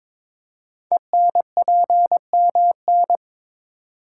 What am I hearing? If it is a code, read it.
Morse code, heard as a keyed tone, about 23 wpm, 700 Hz. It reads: ENPMN
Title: Code Groups - 5 Letter:Numbers - 22wpm